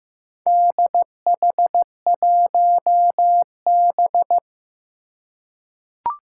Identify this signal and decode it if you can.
Morse code, heard as a keyed tone, about 15 wpm, 700 Hz. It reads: DH1B
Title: Four Challenging Letter-Number Combinations 15wpm